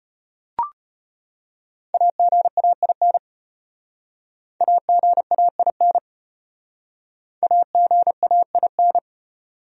Morse code, heard as a keyed tone, about 33 wpm, 700 Hz. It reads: AGAIN AGAIN AGAIN